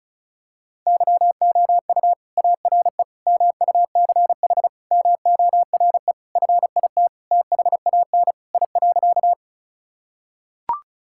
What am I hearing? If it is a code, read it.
Morse code, heard as a keyed tone, about 34 wpm, 690 Hz. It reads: YOU ARE MUCH MORE FIT THAN I.
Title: Sentences from Top 1000 Words - Review Part 5 - 35wpm